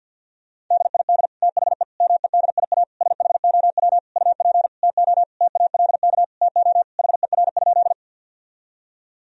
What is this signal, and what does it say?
Morse code, heard as a keyed tone, about 50 wpm, 700 Hz. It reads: DID THE MEDIA SHOW UP TO TALK TO HER?
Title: Sentences from Top 1000 Words - Review Part 4 - 50wpm